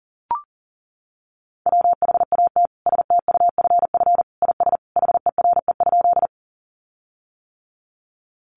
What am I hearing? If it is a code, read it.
Morse code, heard as a keyed tone, about 40 wpm, 700 Hz. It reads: WHAT STUFF IS HERE?
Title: Sentences from Top 1400 Words 40wpm